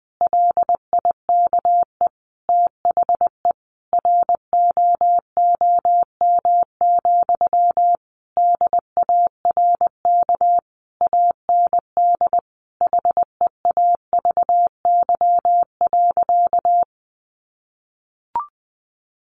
Morse code, heard as a keyed tone, about 20 wpm, 700 Hz. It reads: LIKE THE ROOM, DARK AND HEAVY.
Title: Sentences from Top 1400 Words 20wpm